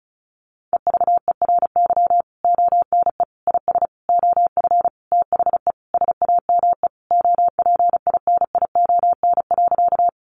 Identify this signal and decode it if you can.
Morse code, heard as a keyed tone, about 34 wpm, 700 Hz. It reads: EVERY ONE IS OF THE SAME OPINION.
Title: Sentences from Top 1200 Words 35wpm